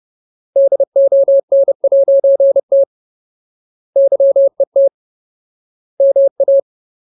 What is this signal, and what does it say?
Morse code, heard as a keyed tone, about 30 wpm, 550 Hz. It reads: DON'T YET MA
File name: Ultimate Word List / 30WPM Ultimate